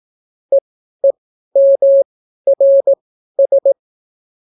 Morse code, heard as a keyed tone, about 18 wpm, spaced as for 12 wpm, 550 Hz. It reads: EEMRS